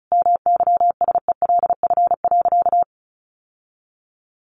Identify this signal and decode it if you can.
Morse code, heard as a keyed tone, about 34 wpm, 700 Hz. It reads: MYSELF.
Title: Sentences from Top 1300 Words 35wpm